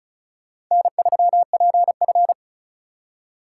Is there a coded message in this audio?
Yes: Morse code from 0.7 to 2.3 s, heard as a keyed tone, about 44 wpm, 700 Hz. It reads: N3PF